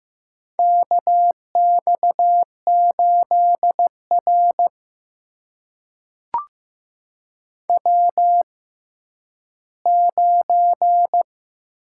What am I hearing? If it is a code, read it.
Morse code, heard as a keyed tone, about 15 wpm, spaced as for 10 wpm, 700 Hz. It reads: KX8R W9
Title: USA Call Signs - 4 Characters - 6x Character Spacing - 15wpm